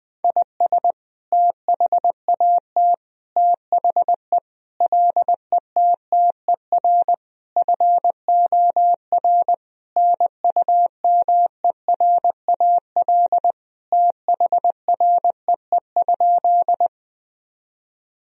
Morse code, heard as a keyed tone, about 20 wpm, 700 Hz. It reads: IS THAT THE LETTER FOR NUMERAL THREE?
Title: Sentences from Top 1000 Words - Review Part 2 - 20wpm